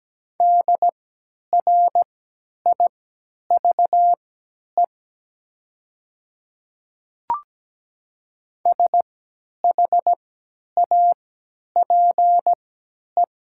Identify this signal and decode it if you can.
Morse code, heard as a keyed tone, about 17 wpm, spaced as for 10 wpm, 700 Hz. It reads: DRIVE SHAPE
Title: 3x Character Spacing - Top 500 Words - 17wpm